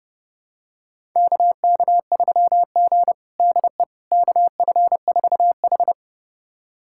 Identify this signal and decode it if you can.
Morse code, heard as a keyed tone, about 30 wpm, 700 Hz. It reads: KK3G DE KF4H
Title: USA Callsign DE Callsign - 4 Characters - 30wpm